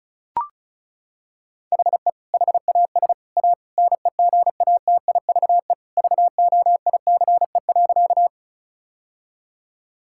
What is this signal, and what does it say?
Morse code, heard as a keyed tone, about 36 wpm, 700 Hz. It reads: HE HAS A NEGATIVE VOICE.
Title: Sentences from Top 1000 Words - Review Part 6 - 35wpm